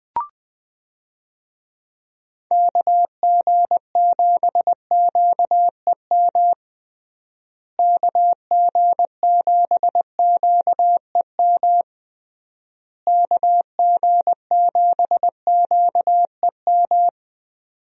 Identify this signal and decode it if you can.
Morse code, heard as a keyed tone, about 20 wpm, 700 Hz. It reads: KG7QEM KG7QEM KG7QEM
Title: USA Call Signs - 6 Characters - Repeated 3x - 20wpm